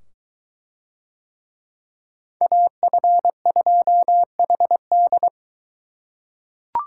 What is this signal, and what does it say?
Morse code, heard as a keyed tone, about 22 wpm, 700 Hz. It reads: AF2HD